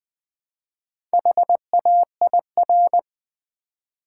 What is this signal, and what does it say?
Morse code, heard as a keyed tone, about 20 wpm, 700 Hz. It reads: HAIR
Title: Build A Word - 1 Letter At A Time - 4-Letter Words - 20wpm